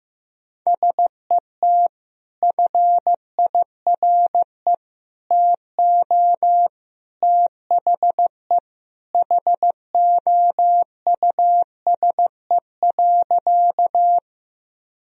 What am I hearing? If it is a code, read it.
Morse code, heard as a keyed tone, about 15 wpm, 700 Hz. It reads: SET FIRE TO THE HOUSE.
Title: Sentences from Top 1000 Words - Review Part 1 - 15wpm